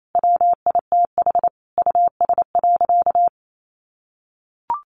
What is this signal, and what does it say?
Morse code, heard as a keyed tone, about 28 wpm, 700 Hz. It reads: WITH US.
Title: A Sound Of Thunder 28wpm